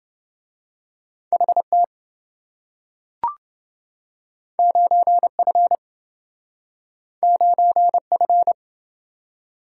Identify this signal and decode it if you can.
Morse code, heard as a keyed tone, about 28 wpm, 700 Hz. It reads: HT 9F 9F